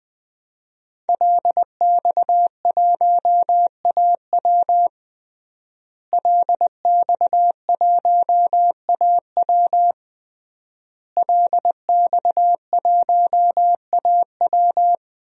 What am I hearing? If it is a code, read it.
Morse code, heard as a keyed tone, about 20 wpm, 700 Hz. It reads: LX1AW LX1AW LX1AW
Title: DXCC-100 Call Signs - 5 Characters - Repeated 3x - 20wpm